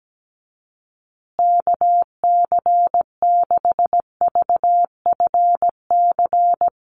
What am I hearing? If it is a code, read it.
Morse code, heard as a keyed tone, about 17 wpm, 700 Hz. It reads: KC6VFC
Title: USA Call Signs Repeated 2x 17wpm